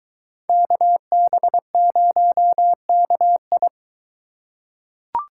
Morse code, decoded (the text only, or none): KB0KI